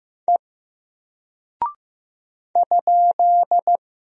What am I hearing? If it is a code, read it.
Morse code, heard as a keyed tone, about 15 wpm, 700 Hz. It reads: E ?
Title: Single Character (Letter:Number:Punctuation:Prosign) 15wpm